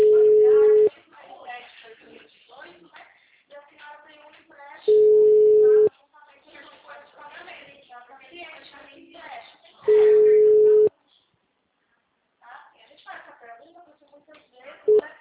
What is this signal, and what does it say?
Ringback tone (Europe, 425 Hz), with speech.